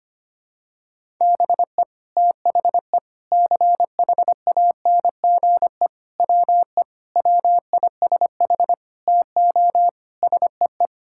Morse code, heard as a keyed tone, about 25 wpm, 700 Hz. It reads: BE THE CHANGE WE WISH TO SEE
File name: practice / 25wpm / Sentences with 9-Word Phrases - 25wpm